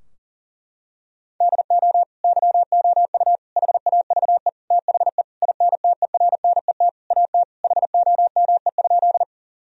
Morse code, heard as a keyed tone, about 40 wpm, 700 Hz. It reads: DO YOU HAVE THE INTERNET AT HOME?
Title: Sentences from Top 1000 Words - Review Part 4 - 40wpm